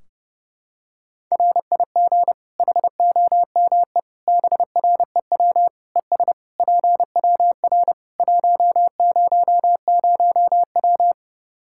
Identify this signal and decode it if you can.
Morse code, heard as a keyed tone, about 30 wpm, 700 Hz. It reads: RIG HOME BREW ES PWR 100W
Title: Practice QSO - Exchange 2 - Info - 30wpm